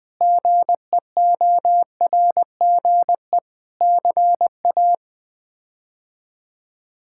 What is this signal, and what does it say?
Morse code, heard as a keyed tone, about 20 wpm, 700 Hz. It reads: GEORGE CA
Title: K1USN SST QSO Practice - 20wpm